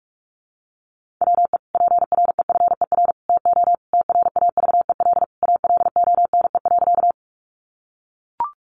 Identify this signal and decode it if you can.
Morse code, heard as a keyed tone, about 44 wpm, 700 Hz. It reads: WE PREFER TO TRAVEL ALONE.
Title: Sentences from Top 1100 Words 45wpm